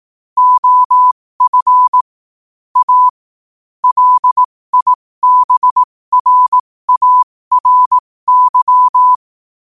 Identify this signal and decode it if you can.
Morse code, heard as a keyed tone, about 18 wpm, spaced as for 15 wpm, 1000 Hz. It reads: OF A LIBRARY
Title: 15 WPM morse code quotes for Thu, 29 Feb 2024